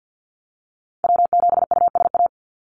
Morse code, heard as a keyed tone, about 50 wpm, 700 Hz. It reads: W7USA
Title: NA SOTA Activator Callsigns 50wpm